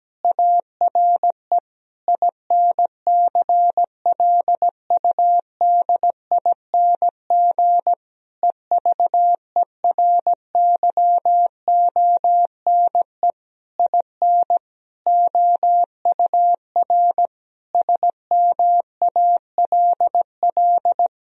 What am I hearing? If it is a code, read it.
Morse code, heard as a keyed tone, about 17 wpm, 700 Hz. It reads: ARE INCLUDING EVERYONE IN OUR SMALL
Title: Sentences from Top 1000 Words - Review Part 4 - 17wpm